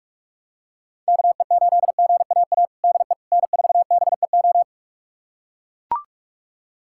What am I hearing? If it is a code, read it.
Morse code, heard as a keyed tone, about 41 wpm, 700 Hz. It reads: KE8GAA DE N4BEO
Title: USA Callsign DE Callsign 45wpm